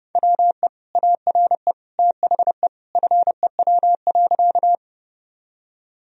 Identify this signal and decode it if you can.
Morse code, heard as a keyed tone, about 32 wpm, 700 Hz. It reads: WE ARE THE FEW.
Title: Sentences from Top 1000 Words - Review Part 1 - 30wpm